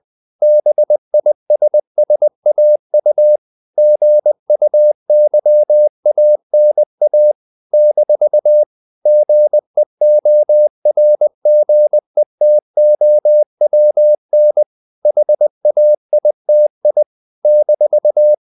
Morse code, heard as a keyed tone, about 20 wpm, 600 Hz. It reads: BISSAU GUYANA - GEORGETOWN HAITI -
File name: CW / Practice / NationalCapitals20